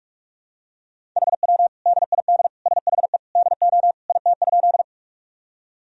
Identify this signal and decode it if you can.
Morse code, heard as a keyed tone, about 46 wpm, 700 Hz. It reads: HW DID SHE DO IT?
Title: Sentences from Top 100 Words and Abbreviations 45wpm